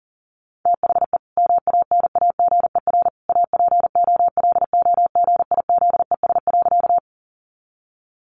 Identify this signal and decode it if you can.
Morse code, heard as a keyed tone, about 40 wpm, 700 Hz. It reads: THE MANAGER APOLOGIZES.
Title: Additional Easy Phrases 40wpm